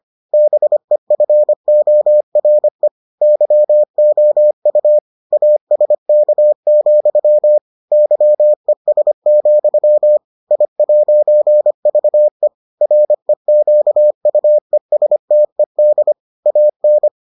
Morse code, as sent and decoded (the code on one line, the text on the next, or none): -... ..-. --- .-. / -.-- --- ..- / .- ... -.- --..-- / -.-- . ... --..-- / .. .----. ...- . / .-. --.- ..- . ... - . -.. / .- -.
BEFORE YOU ASK, YES, I'VE REQUESTED AN